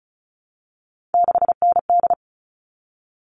Morse code, heard as a keyed tone, about 34 wpm, 700 Hz. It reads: 6ND
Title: All Cut Numbers - 3-Digit 35wpm